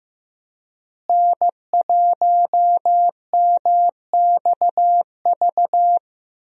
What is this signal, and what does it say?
Morse code, heard as a keyed tone, about 15 wpm, 700 Hz. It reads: N1MXV
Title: USA Call Signs - 5 Characters - Repeated 3x - 15wpm